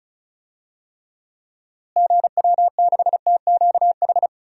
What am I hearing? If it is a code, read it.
Morse code, heard as a keyed tone, about 36 wpm, 700 Hz. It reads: GW6TQH
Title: DXCC-100 Call Signs - 6 Characters - Repeated 2x - 35wpm